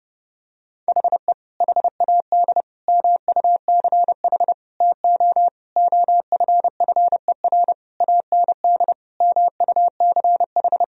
Morse code, heard as a keyed tone, about 30 wpm, 700 Hz. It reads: HE HAD MUCH TO OFFER AND MUCH
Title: Sentences from Top 1000 Words - Review Part 4 - 30wpm